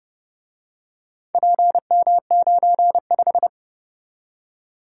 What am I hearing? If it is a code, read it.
Morse code, heard as a keyed tone, about 29 wpm, 700 Hz. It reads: PM95